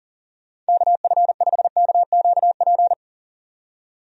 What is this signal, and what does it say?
Morse code, heard as a keyed tone, about 37 wpm, 700 Hz. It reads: KF5KQP
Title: USA Call Signs - 6 Characters - 40wpm